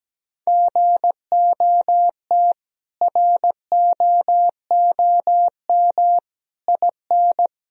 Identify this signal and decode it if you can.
Morse code, heard as a keyed tone, about 17 wpm, 700 Hz. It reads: GOT ROOM IN
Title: Sentences from Top 1000 Words - Review Part 1 - 17wpm